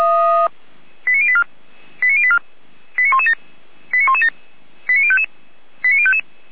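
Five-tone selective call (ZVEI), heard as quick runs of five tones, tones of about 70 ms. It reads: 89084 89084 89117 89117 89055 89055
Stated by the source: ZVEI five-tone selective call 89094, received on SDR.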